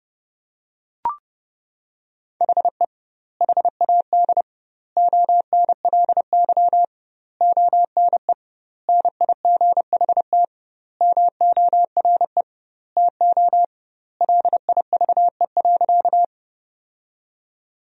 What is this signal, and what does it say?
Morse code, heard as a keyed tone, about 32 wpm, 700 Hz. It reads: HE HAD ONLY ONE NIGHT MORE TO LIVE.
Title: 2x Word-Spacing - Sentences from Top 300 Words - 30wpm